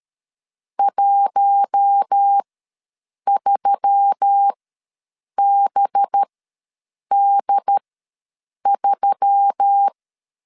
Morse code, heard as a keyed tone, about 13 wpm, spaced as for 7 wpm, 800 Hz. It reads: 13BD3
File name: 5 LB ws 12wpm